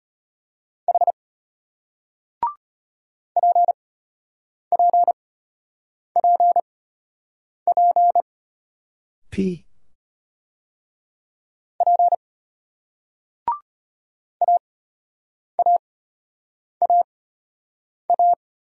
Morse code, heard as a keyed tone, about 33 wpm, spaced as for 8 wpm, 700 Hz. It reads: H PPPP P AAAA